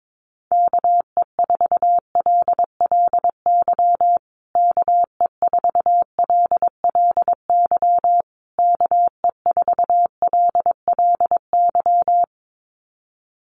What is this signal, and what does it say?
Morse code, heard as a keyed tone, about 22 wpm, 700 Hz. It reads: KE4LLY KE4LLY KE4LLY
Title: USA Call Signs Repeated 3x 22wpm